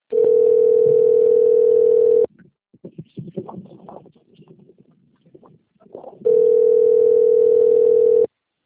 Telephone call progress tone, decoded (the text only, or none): ringback tone